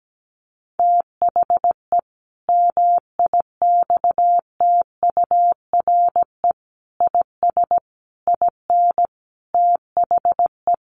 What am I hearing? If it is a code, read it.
Morse code, heard as a keyed tone, about 17 wpm, 700 Hz. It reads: THE MIXTURE IS IN THE
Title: Sentences from Top 1500 Words 17wpm